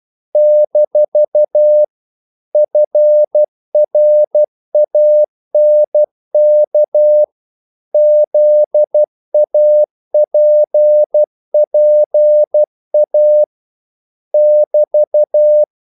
Morse code, heard as a keyed tone, about 12 wpm, 600 Hz. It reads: - FRANK ZAPPA <BT>
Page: English quotations in CW (telegraphy - morse)